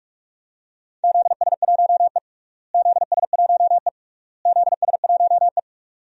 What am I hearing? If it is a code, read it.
Morse code, heard as a keyed tone, about 42 wpm, 700 Hz. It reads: ZS1E ZS1E ZS1E
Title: DXCC-100 Call Signs - 4 Characters - Repeated 3x - 45wpm